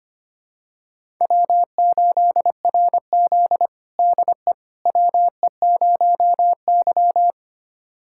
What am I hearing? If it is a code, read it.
Morse code, heard as a keyed tone, about 25 wpm, 700 Hz. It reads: W8RZ DE WE0Y
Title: USA Callsign DE Callsign - 4 Characters - 25wpm